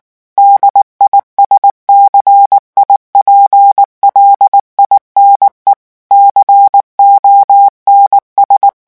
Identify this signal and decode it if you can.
Morse code, heard as a keyed tone, about 19 wpm, 800 Hz. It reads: DISCIPLINE CONS
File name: morse mp3 / revision9 citations humoristiques vitesse 18 mots minute